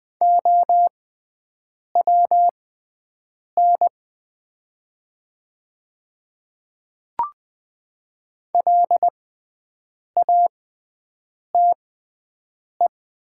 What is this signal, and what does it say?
Morse code, heard as a keyed tone, about 20 wpm, spaced as for 7 wpm, 700 Hz. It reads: OWN LATE
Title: 6x Character Spacing - Top 300 Words - 20wpm